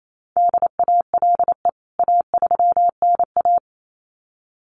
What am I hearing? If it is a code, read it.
Morse code, heard as a keyed tone, about 28 wpm, 700 Hz. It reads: DALE A3NA
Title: CWT Exchange 28wpm